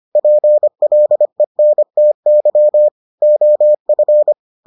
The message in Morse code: PLENTY OF

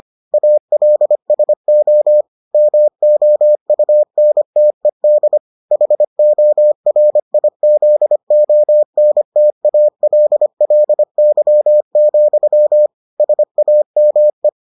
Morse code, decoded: ALSO MOUNTED HORIZONTALLY, SAME

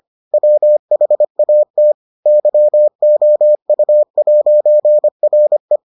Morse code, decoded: WHAT YOU'RE

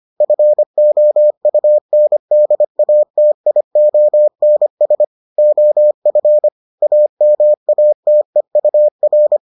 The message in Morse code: FOUNDATIONS OF AMATEUR

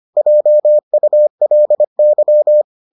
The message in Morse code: JULY

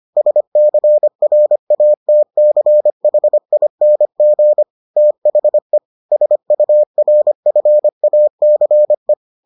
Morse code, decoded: SCRATCHING THE SURFACE

